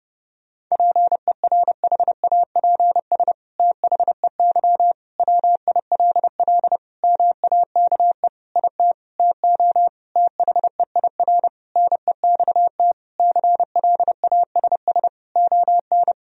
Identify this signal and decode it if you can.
Morse code, heard as a keyed tone, about 30 wpm, 700 Hz. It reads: PERHAPS THEY WILL MAKE IT TO THEIR NEXT CLASS ON
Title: Sentences from Top 1000 Words - Review Part 2 - 30wpm